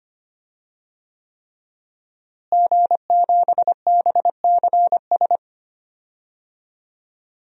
Morse code, heard as a keyed tone, about 25 wpm, 700 Hz. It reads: G7BCS
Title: DXCC-100 Call Signs - 5 Characters - Repeated 3x - 25wpm